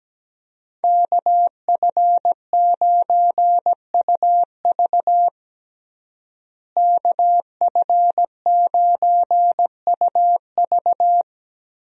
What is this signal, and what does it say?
Morse code, heard as a keyed tone, about 17 wpm, 700 Hz. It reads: KF9UV KF9UV
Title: USA Call Signs - 5 Characters - Repeated 3x - 17wpm